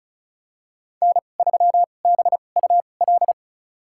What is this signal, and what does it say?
Morse code, heard as a keyed tone, about 34 wpm, spaced as for 25 wpm, 700 Hz. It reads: N3BUL
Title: USA Call Signs - 5 Characters - 2x Character Spacing - 35wpm